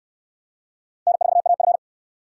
Morse code, heard as a keyed tone, about 64 wpm, 700 Hz. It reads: N4AF